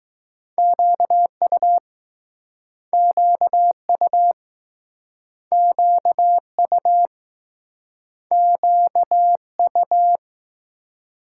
Morse code, heard as a keyed tone, about 19 wpm, 700 Hz. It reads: QU QU QU QU